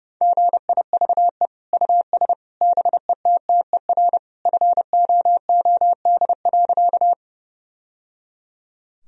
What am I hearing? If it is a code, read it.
Morse code, heard as a keyed tone, about 30 wpm, 700 Hz. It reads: GIVE US BETTER FOOD.
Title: Sentences from Top 1000 Words - Review Part 1 - 30wpm